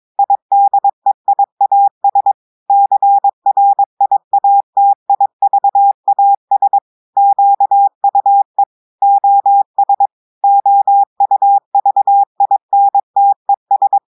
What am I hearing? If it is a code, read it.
Morse code, heard as a keyed tone, about 22 wpm, 800 Hz. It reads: IDEIAS CRIATIVAS QUE OS OUVINTES
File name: audios / portugais / portugais texte en clair 2 vitesse 22 mots minutes